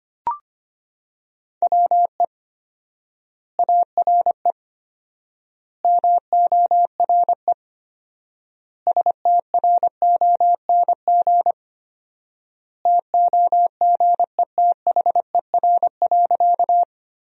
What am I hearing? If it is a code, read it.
Morse code, heard as a keyed tone, about 25 wpm, 700 Hz. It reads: WE ARE MORE STRONG TOGETHER.
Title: 4x Word-Spacing - Sentences from Top 500 Words - 25wpm